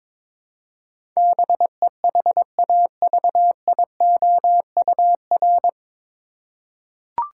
Morse code, heard as a keyed tone, about 22 wpm, 700 Hz. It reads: BEHAVIOUR
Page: Morse Code Ninja practice 22wpm